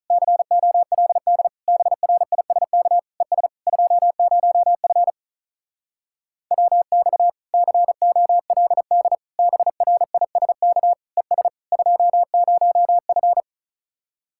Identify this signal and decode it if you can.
Morse code, heard as a keyed tone, about 45 wpm, 700 Hz. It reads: COLD BRISK ES 20F WX COLD BRISK ES 20F